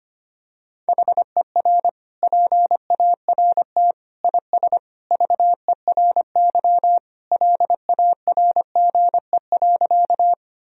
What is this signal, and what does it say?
Morse code, heard as a keyed tone, about 25 wpm, 700 Hz. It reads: HER PART IS VERY LARGE.
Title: Sentences from Top 1000 Words - Review Part 1 - 25wpm